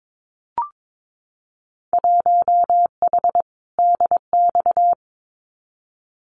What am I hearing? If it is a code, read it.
Morse code, heard as a keyed tone, about 22 wpm, 700 Hz. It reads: 1H DX
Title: Winter Field Day Exchanges 22wpm